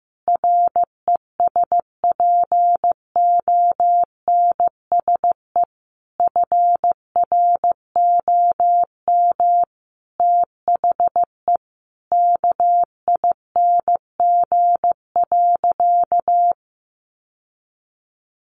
Morse code, heard as a keyed tone, about 15 wpm, 700 Hz. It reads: RESPONSE FROM THE KING.
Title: Sentences from Top 1400 Words 15wpm